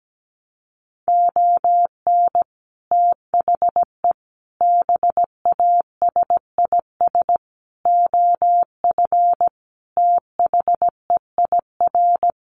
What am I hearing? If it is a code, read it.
Morse code, heard as a keyed tone, about 17 wpm, 700 Hz. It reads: ON THE BASIS OF THEIR
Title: Sets of 5 Words Encore 17wpm